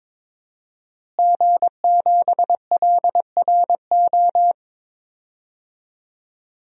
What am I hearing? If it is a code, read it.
Morse code, heard as a keyed tone, about 22 wpm, 700 Hz. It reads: G7LRO
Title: DXCC-100 Call Signs - 5 Characters - Repeated 3x - 22wpm